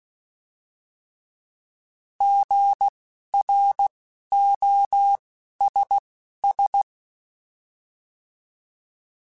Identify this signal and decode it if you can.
Morse code, heard as a keyed tone, about 16 wpm, spaced as for 11 wpm, 800 Hz. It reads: GROSS
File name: Alphabetical / Alphabetical mp3s / Gs @10